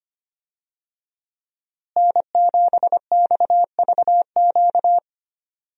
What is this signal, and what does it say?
Morse code, heard as a keyed tone, about 25 wpm, 700 Hz. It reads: N7XVQ
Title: USA Call Signs - 5 Characters - Repeated 2x - 25wpm